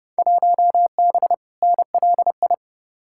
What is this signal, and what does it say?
Morse code, heard as a keyed tone, about 30 wpm, 700 Hz. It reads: 1B NLI
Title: Field Day Exchanges - 30wpm